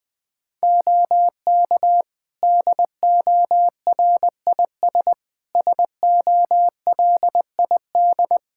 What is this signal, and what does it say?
Morse code, heard as a keyed tone, about 20 wpm, 700 Hz. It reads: OK DORIS SOLID
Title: Practice QSO - Exchange 3 - More Info - 20wpm